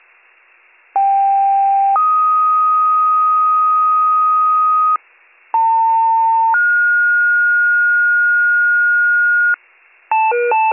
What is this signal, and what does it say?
Two-tone page, heard as a one-second tone, then a three-second tone, about 790 then 1250 Hz, about 900 then 1430 Hz, with noise. It is heard from 0.9 to 9.5 s.